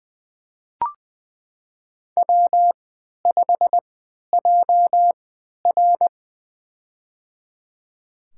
Morse code, heard as a keyed tone, about 20 wpm, spaced as for 11 wpm, 700 Hz. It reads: W5JR
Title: USA Call Signs - 4 Characters - 3x Character Spacing - 20wpm